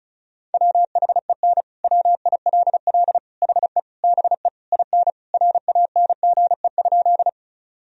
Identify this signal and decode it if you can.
Morse code, heard as a keyed tone, about 34 wpm, 700 Hz. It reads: WHEN WILL HE BE IN RANGE?
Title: Sentences from Top 1000 Words - Review Part 4 - 35wpm